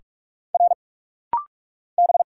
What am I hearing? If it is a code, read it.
Morse code, heard as a keyed tone, about 44 wpm, 700 Hz. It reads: R B
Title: Single Letter-Number - Rapid-Fire 45wpm